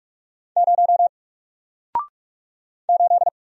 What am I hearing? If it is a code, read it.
Morse code, heard as a keyed tone, about 47 wpm, 700 Hz. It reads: 0 8
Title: Single Character (Letter:Number:Punctuation:Prosign) Alt 45wpm